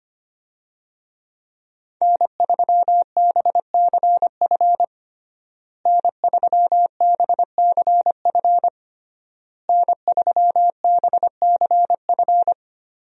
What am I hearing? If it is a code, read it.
Morse code, heard as a keyed tone, about 25 wpm, 700 Hz. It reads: N3BCF N3BCF N3BCF
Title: USA Call Signs - 5 Characters - Repeated 3x - 25wpm